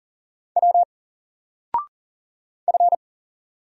Morse code, heard as a keyed tone, about 40 wpm, 700 Hz. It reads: W F
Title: Single Character (Letter:Number:Punctuation:Prosign) 40wpm